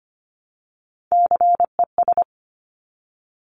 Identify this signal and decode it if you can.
Morse code, heard as a keyed tone, about 25 wpm, 700 Hz. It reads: CES
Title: Top Three Letter Combinations in Words 25wpm